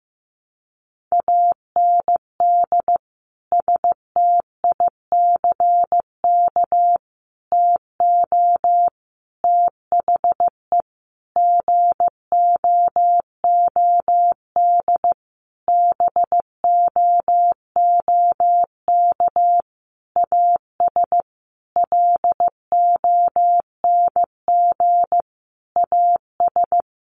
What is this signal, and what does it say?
Morse code, heard as a keyed tone, about 15 wpm, 700 Hz. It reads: AND STICK TO THE GOOD BOOK AS LONG AS
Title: Sentences from Top 1200 Words 15wpm